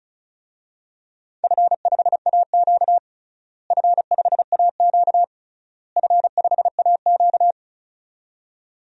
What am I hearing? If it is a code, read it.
Morse code, heard as a keyed tone, about 34 wpm, 700 Hz. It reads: F5AQ F5AQ F5AQ
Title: DXCC-100 Call Signs - 4 Characters - Repeated 3x - 35wpm